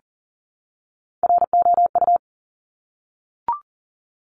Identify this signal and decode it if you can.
Morse code, heard as a keyed tone, about 42 wpm, 700 Hz. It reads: ROU
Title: Top Three Letter Combinations in Words 40wpm